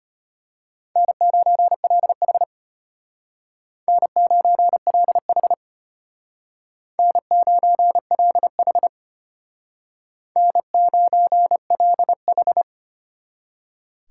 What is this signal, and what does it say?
Morse code, heard as a keyed tone, about 33 wpm, 700 Hz. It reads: N9LH N9LH N9LH N9LH